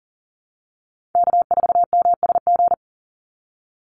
Morse code, heard as a keyed tone, about 38 wpm, 700 Hz. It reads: K4MSG
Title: Call Signs Repeated 2x - 40wpm